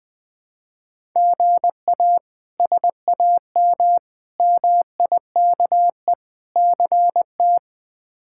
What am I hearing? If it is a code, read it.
Morse code, heard as a keyed tone, about 21 wpm, 700 Hz. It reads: GA SAM MIKE CT
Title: K1USN SST QSO Practice - 20wpm